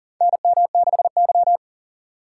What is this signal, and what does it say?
Morse code, heard as a keyed tone, about 40 wpm, 700 Hz. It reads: NM6Y
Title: USA Call Signs - 4 Characters - 4x Character Spacing - 40wpm